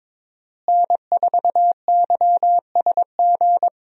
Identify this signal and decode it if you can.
Morse code, heard as a keyed tone, about 22 wpm, 700 Hz. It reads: N4YSG
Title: USA Call Signs - 5 Characters - Repeated 3x - 22wpm